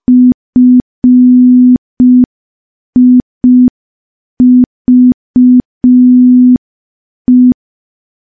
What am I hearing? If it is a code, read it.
Morse code, heard as a keyed tone, about 5 wpm, 250 Hz. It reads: FIVE
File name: morse / morse-training-05WPM